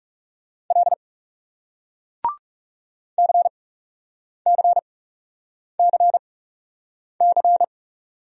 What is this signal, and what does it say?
Morse code, heard as a keyed tone, about 39 wpm, spaced as for 8 wpm, 700 Hz. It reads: L CCCC